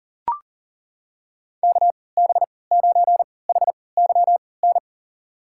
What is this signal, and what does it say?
Morse code, heard as a keyed tone, about 40 wpm, spaced as for 23 wpm, 700 Hz. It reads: KB9HYN
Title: USA Call Signs - 6 Characters - 3x Character Spacing - 40wpm